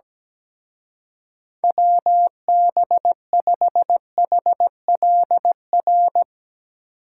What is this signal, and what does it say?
Morse code, heard as a keyed tone, about 18 wpm, 700 Hz. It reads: WB5HLR
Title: Speed Racing - USA 2x3 Call Signs 17wpm